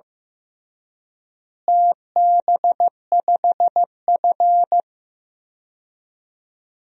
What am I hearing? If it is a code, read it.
Morse code, heard as a keyed tone, about 16 wpm, 700 Hz. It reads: TB5F
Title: DXCC-100 Call Signs - 4 Characters - Repeated 3x - 15wpm